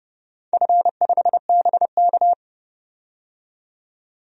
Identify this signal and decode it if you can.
Morse code, heard as a keyed tone, about 30 wpm, 700 Hz. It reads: F5BK
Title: DXCC-100 Call Signs - 4 Characters - 30wpm